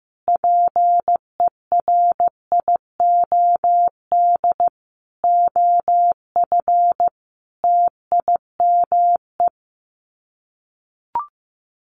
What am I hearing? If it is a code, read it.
Morse code, heard as a keyed tone, about 15 wpm, 700 Hz. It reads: PERIOD OF TIME
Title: Sets of 5 Words 15wpm